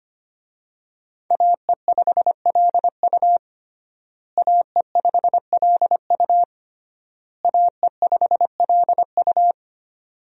Morse code, heard as a keyed tone, about 25 wpm, 700 Hz. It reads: AE5LU AE5LU AE5LU
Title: USA Call Signs - 5 Characters - Repeated 3x - 25wpm